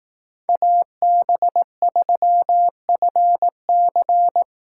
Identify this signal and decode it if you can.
Morse code, heard as a keyed tone, about 18 wpm, 700 Hz. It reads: AB3FC
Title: Speed Racing - USA 2x2 Call Signs 15wpm